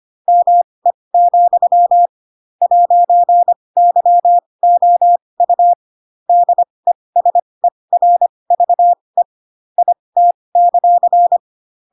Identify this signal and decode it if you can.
Morse code, heard as a keyed tone, about 25 wpm, spaced as for 20 wpm, 700 Hz. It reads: ME, 'YOU DESERVE IT;
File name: Pinocchio / Pinocchio ch7 15 25 ik1hge